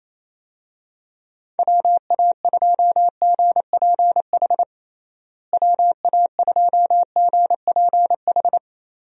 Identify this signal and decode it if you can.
Morse code, heard as a keyed tone, about 28 wpm, 700 Hz. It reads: WA2GPH WA2GPH
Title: USA Call Signs - 6 Characters - Repeated 3x - 28wpm